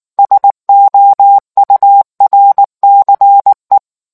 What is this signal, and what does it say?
Morse code, heard as a keyed tone, about 19 wpm, 800 Hz. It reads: SOURCE